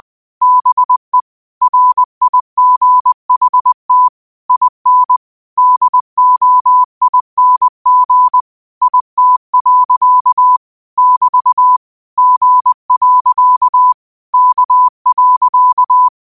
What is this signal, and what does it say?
Morse code, heard as a keyed tone, about 20 wpm, 1000 Hz. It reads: BE RIGHT IN DOING IT. <BT> G. K.
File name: qotdfiles / 20wpm